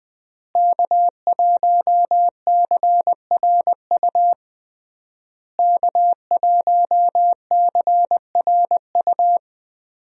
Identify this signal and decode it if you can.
Morse code, heard as a keyed tone, about 20 wpm, 700 Hz. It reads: K1CRU K1CRU
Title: USA Call Signs - 5 Characters - Repeated 3x - 20wpm